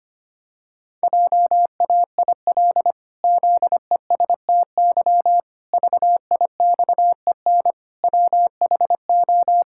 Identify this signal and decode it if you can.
Morse code, heard as a keyed tone, about 25 wpm, 700 Hz. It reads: JAIL ZESTY VIXEN WHO
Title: Pangram Challenge v2 - 25wpm